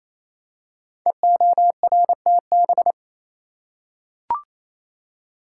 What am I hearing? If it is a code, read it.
Morse code, heard as a keyed tone, about 27 wpm, 700 Hz. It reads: EORTB
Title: Code Groups - 5 Letter:Numbers - 28wpm